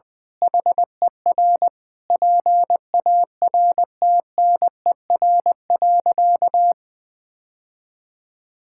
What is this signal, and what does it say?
Morse code, heard as a keyed tone, about 21 wpm, 700 Hz. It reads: HER PARTNER.
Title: Sentences with 8-Word Phrases - 20wpm